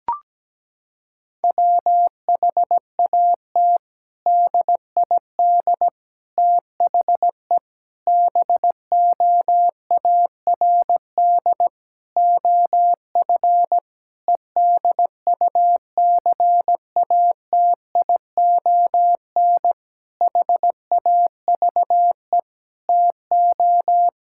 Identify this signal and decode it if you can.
Morse code, heard as a keyed tone, about 17 wpm, 700 Hz. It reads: WHAT DID THE BOARD OF EDUCATION HAVE TO
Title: Sentences from Top 1000 Words - Review Part 6 - 17wpm